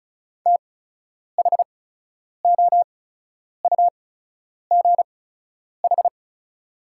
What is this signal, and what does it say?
Morse code, heard as a keyed tone, about 34 wpm, spaced as for 10 wpm, 690 Hz. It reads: THOUGH
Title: 8x Character Spacing - Top 500 Words - 35wpm